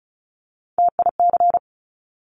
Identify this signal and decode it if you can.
Morse code, heard as a keyed tone, about 36 wpm, 700 Hz. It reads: TIC
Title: Top Three Letter Combinations in Words 35wpm